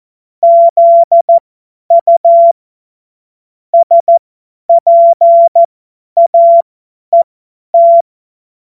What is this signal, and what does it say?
Morse code, heard as a keyed tone, about 14 wpm, spaced as for 10 wpm, 680 Hz. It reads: ZU SPAET
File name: Morsekurs / Witze / Witz 16. Mai 2023 50 F70